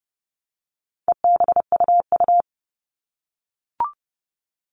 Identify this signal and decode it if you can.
Morse code, heard as a keyed tone, about 30 wpm, 700 Hz. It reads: EBUU
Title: All Cut Numbers - 4-Digit 30wpm